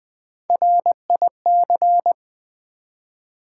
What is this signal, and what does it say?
Morse code, heard as a keyed tone, about 20 wpm, 690 Hz. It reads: RIC
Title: Build A Word - 1 Letter At A Time - 4-Letter Words - 20wpm